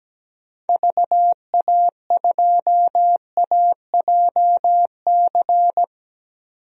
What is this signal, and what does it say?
Morse code, heard as a keyed tone, about 17 wpm, 700 Hz. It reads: VA2AJC
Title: DXCC-100 Call Signs - 6 Characters - 17wpm